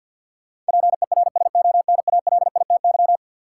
Morse code, heard as a keyed tone, about 53 wpm, 700 Hz. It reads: PERSONALITY
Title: Sentences from Top 1400 Words - New Word Drill - 50wpm